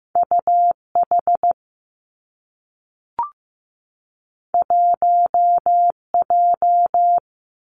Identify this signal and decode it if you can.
Morse code, heard as a keyed tone, about 15 wpm, 700 Hz. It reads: UH 1J
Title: Two Challenging Letter-Number Combinations 15wpm